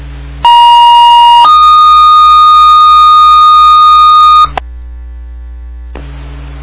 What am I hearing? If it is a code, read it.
Two-tone page, heard as a one-second tone, then a three-second tone, about 950 then 1230 Hz. It is heard from 0.4 to 4.4 s.